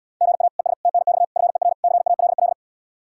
Morse code, heard as a keyed tone, about 77 wpm, 700 Hz. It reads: GA ES TNX FER CALL